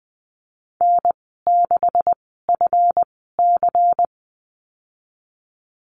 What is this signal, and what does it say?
Morse code, heard as a keyed tone, about 19 wpm, spaced as for 14 wpm, 700 Hz. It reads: N6FC
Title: 167 Lesson 27g - Number 7 - Call Signs 20wpm